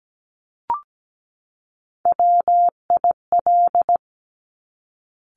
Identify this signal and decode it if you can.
Morse code, heard as a keyed tone, about 17 wpm, 700 Hz. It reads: WIL
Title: Top Three Letter Combinations in Words 17wpm